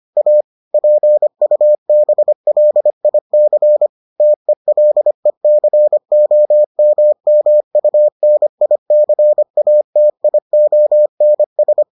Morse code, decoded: A PUBLIC TELECOMMUNICATIONS

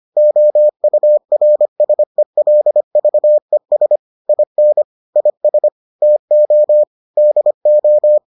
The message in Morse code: OURSELVES IN IS TO DO

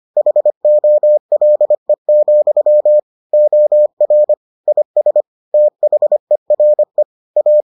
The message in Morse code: HOLE, OR IS THERE A

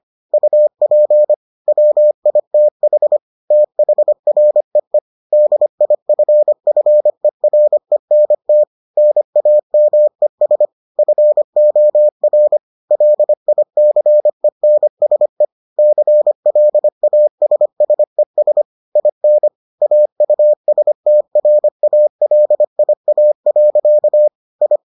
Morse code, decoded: UP WITH THREE DIFFERENT NAMES FOR LICENSE CLASSES IN AUSTRALIA. I

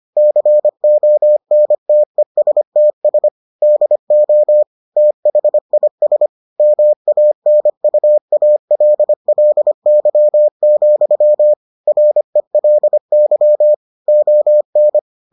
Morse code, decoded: CONTESTS DO THIS MANUALLY, RELY ON

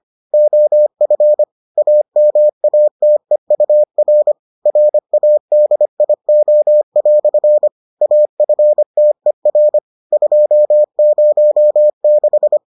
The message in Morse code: OF AMATEUR RADIO" AFTER 206

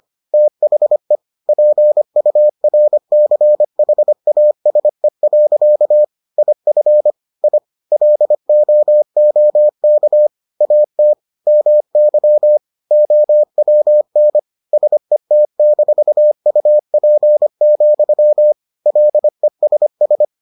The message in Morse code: THE PURCHASE. IF I LOOK AT MY OWN SET-UP, LESS

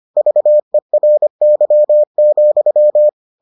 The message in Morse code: VERY,

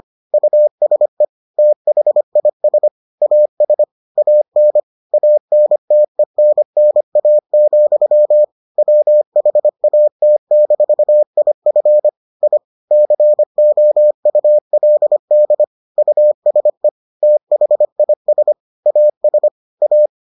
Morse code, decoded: USE THIS AS AN ANTENNA, WHAT-IF I COULD USE THIS AS A

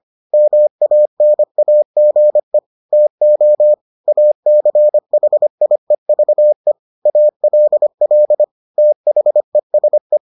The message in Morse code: MANAGE TO ACHIEVE ALL THESE

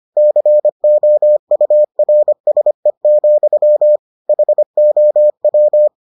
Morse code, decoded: COURSE, HOW

